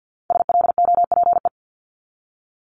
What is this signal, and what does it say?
Morse code, heard as a keyed tone, about 50 wpm, 690 Hz. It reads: SLOPE
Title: Multi-letter Phonemes Repeated 3x 50wpm